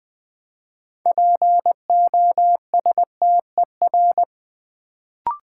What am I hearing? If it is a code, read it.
Morse code, heard as a keyed tone, about 20 wpm, 700 Hz. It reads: POSTER
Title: Build A Word - 3 Letters At A Time - 6-Letter Words - 20wpm